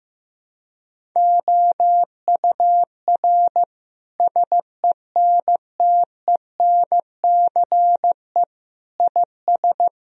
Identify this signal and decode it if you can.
Morse code, heard as a keyed tone, about 15 wpm, 700 Hz. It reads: OUR SENTENCE IS
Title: Sentences from Top 1000 Words - Review Part 1 - 15wpm